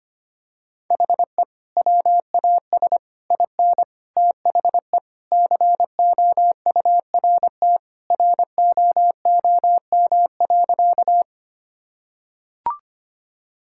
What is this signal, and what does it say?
Morse code, heard as a keyed tone, about 25 wpm, 700 Hz. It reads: HE WAS IN THE COURT ROOM.
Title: Sentences from Top 1600 Words 25wpm